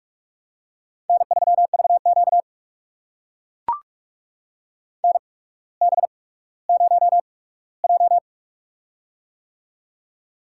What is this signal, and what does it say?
Morse code, heard as a keyed tone, about 44 wpm, spaced as for 19 wpm, 700 Hz. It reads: N3VQ NB0J
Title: USA Call Signs - 4 Characters - 8x Character Spacing - 45wpm